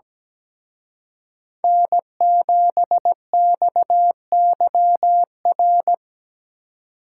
Morse code, heard as a keyed tone, about 18 wpm, 700 Hz. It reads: N7XYR
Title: Speed Racing - USA Call Signs 17wpm